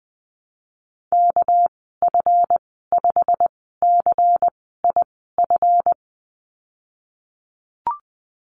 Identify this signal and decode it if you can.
Morse code, heard as a keyed tone, about 19 wpm, spaced as for 14 wpm, 700 Hz. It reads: KF5CIF
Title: 213 Lesson 34g - Number 8 - Call Signs 20wpm